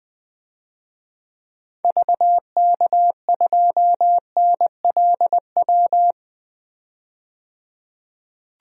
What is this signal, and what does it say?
Morse code, heard as a keyed tone, about 20 wpm, 700 Hz. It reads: VK2NLW
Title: DXCC-100 Call Signs - 6 Characters - 20wpm